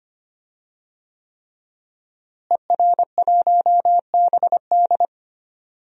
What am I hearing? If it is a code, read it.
Morse code, heard as a keyed tone, about 25 wpm, 700 Hz. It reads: ER1BD
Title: DXCC-100 Call Signs - 5 Characters - Repeated 3x - 25wpm